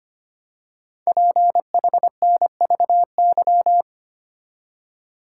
Morse code, heard as a keyed tone, about 25 wpm, 700 Hz. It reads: PHNVY
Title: Code Groups - 5 Letters - 25wpm